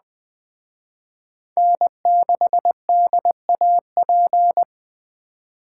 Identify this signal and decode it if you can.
Morse code, heard as a keyed tone, about 20 wpm, 700 Hz. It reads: N6DAP
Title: Speed Racing - USA 1x3 Call Signs 17wpm